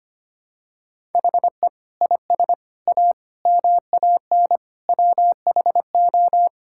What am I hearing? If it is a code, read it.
Morse code, heard as a keyed tone, about 25 wpm, 700 Hz. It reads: HE IS A MAN WHO
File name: practice / 25wpm / Sentences with 3-Word Phrases - 25wpm